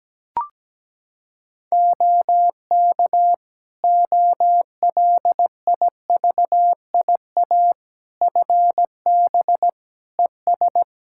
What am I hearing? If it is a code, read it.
Morse code, heard as a keyed tone, about 17 wpm, 700 Hz. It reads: OK OLIVIA FB ES
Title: Practice QSO - Exchange 2 - Info - 17wpm